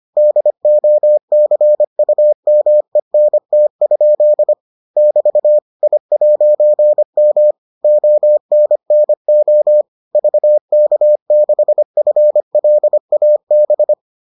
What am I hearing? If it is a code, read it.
Morse code, heard as a keyed tone, about 25 wpm, 600 Hz. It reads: DOCUMENT? <BT> I'M ONNO VK6FLAB